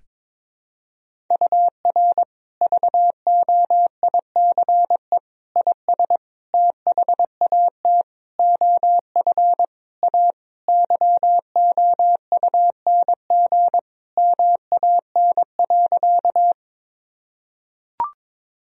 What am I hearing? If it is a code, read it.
Morse code, heard as a keyed tone, about 22 wpm, 700 Hz. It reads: UR VOICE IS THAT OF A YOUNG MAN.
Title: Sentences from Top 500 Words and Abbreviations 22wpm